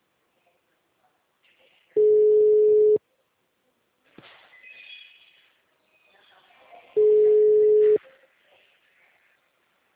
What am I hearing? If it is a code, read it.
Ringback tone (Europe, 425 Hz).
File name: out- -2050- - - .54347